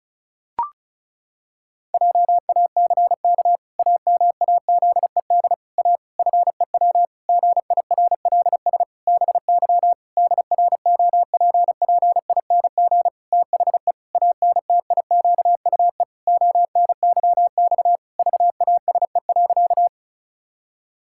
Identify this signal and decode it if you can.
Morse code, heard as a keyed tone, about 36 wpm, 700 Hz. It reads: JACK AMAZED A FEW GIRLS BY DROPPING THE ANTIQUE ONYX VASE.
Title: Pangram Challenge v2 - 35wpm